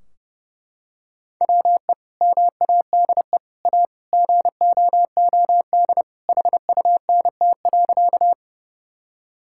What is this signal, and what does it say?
Morse code, heard as a keyed tone, about 32 wpm, 700 Hz. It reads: WE MADE A GOOD HUNT.
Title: Sentences from Top 1600 Words 30wpm